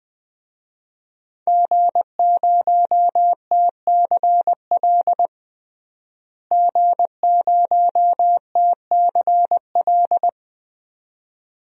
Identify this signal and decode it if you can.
Morse code, heard as a keyed tone, about 20 wpm, 700 Hz. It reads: G0TCL G0TCL
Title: DXCC-100 Call Signs - 5 Characters - Repeated 2x - 20wpm